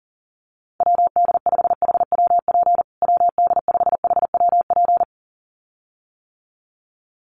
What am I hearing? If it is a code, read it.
Morse code, heard as a keyed tone, about 40 wpm, 700 Hz. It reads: WD5HWP WD5HWP
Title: USA Call Signs Repeated 2x 40wpm